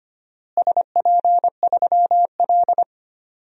Morse code, heard as a keyed tone, about 25 wpm, 700 Hz. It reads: SP3L
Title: DXCC-100 Call Signs - 4 Characters - Repeated 2x - 25wpm